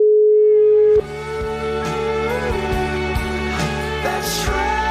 Ringback tone (Europe, 425 Hz), with noise.